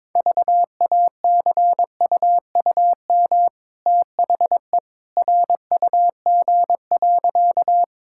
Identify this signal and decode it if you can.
Morse code, heard as a keyed tone, about 22 wpm, 690 Hz. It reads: VACUUM THE RUG.
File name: practice / 22wpm / Easy Sentences with Hints - 22wpm